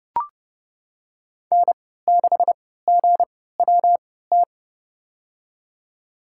Morse code, heard as a keyed tone, about 30 wpm, spaced as for 17 wpm, 700 Hz. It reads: N6GWT
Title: USA Call Signs - 5 Characters - 3x Character Spacing - 30wpm